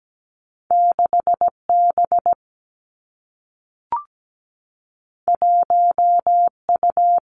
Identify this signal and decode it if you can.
Morse code, heard as a keyed tone, about 17 wpm, 700 Hz. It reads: 6 B 1 U
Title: Two Challenging Letter-Number Combinations 17wpm